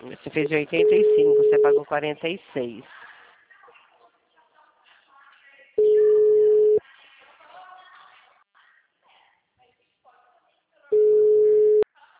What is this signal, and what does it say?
Ringback tone (Europe, 425 Hz), with speech.